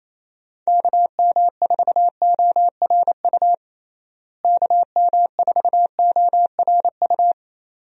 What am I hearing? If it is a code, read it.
Morse code, heard as a keyed tone, about 28 wpm, 700 Hz. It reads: KM4ORU KM4ORU
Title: USA Call Signs - 6 Characters - Repeated 2x - 28wpm